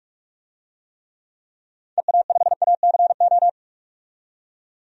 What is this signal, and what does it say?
Morse code, heard as a keyed tone, about 43 wpm, 700 Hz. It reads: EA5ACO
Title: DXCC-100 Call Signs - 6 Characters - 45wpm